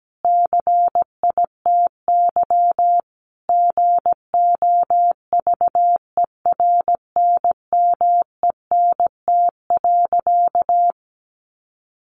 Morse code, heard as a keyed tone, about 17 wpm, 700 Hz. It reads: CITY GOVERNMENT.
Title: Sentences from Top 1400 Words 17wpm